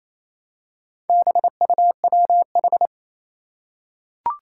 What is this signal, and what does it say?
Morse code, heard as a keyed tone, about 27 wpm, 700 Hz. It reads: BUWH
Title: Four Challenging Letter-Number Combinations 28wpm